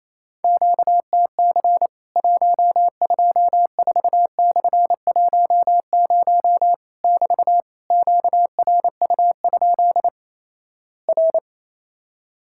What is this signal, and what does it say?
Morse code, heard as a keyed tone, about 28 wpm, 700 Hz. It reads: QTC 124/10 <BT> QRU? R
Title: Worked All Europe DX Contest - QTC Exchange - 28wpm